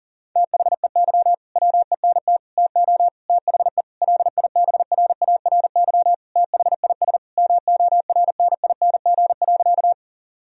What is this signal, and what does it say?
Morse code, heard as a keyed tone, about 40 wpm, 700 Hz. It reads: THEY WENT TO THE LIBRARY THIS MORNING.
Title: Sentences from Top 1000 Words - Review Part 4 - 40wpm